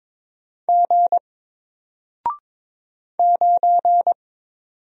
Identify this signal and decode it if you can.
Morse code, heard as a keyed tone, about 22 wpm, 700 Hz. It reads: G 9
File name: Single-Letter-Number-22wpm